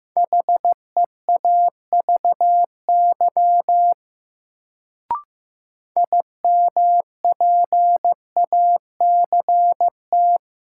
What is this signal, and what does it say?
Morse code, heard as a keyed tone, about 15 wpm, 700 Hz. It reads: HEAVY IMPACT
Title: Sentences from Top 1000 Words - Review Part 6 - New Word Drill - 15wpm